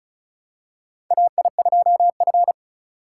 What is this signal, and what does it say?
Morse code, heard as a keyed tone, about 36 wpm, 700 Hz. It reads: AI2F
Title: USA Call Signs - 4 Characters - 2x Character Spacing - 35wpm